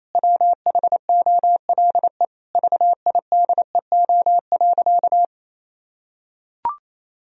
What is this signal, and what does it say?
Morse code, heard as a keyed tone, about 28 wpm, 700 Hz. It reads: WHOLE VIDEO.
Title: Sentences from Top 1000 Words - Review Part 4 - 28wpm